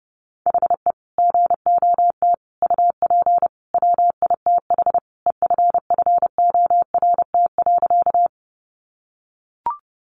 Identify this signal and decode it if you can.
Morse code, heard as a keyed tone, about 30 wpm, 700 Hz. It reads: HE GOT UP WITH EFFORT.
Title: Sentences from Top 1200 Words 30wpm